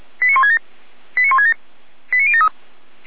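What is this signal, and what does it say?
Five-tone selective call (ZVEI), heard as quick runs of five tones, tones of about 80 ms. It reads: 89167 89167 89073
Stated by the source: ZVEI five-tone call received by SDR, from feuerwehr-orlamuende.org (fire brigade Orlamünde).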